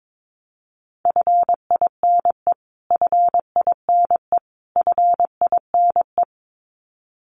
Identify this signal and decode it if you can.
Morse code, heard as a keyed tone, about 22 wpm, 700 Hz. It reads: FINE FINE FINE
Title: Top 500 Words Repeated 22wpm